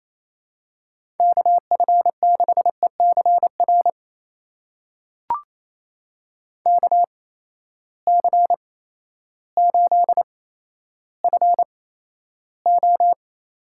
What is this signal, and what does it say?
Morse code, heard as a keyed tone, about 27 wpm, spaced as for 13 wpm, 700 Hz. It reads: KF6ECR KC8FO
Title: USA Call Signs - 6 Characters - 8x Character Spacing - 28wpm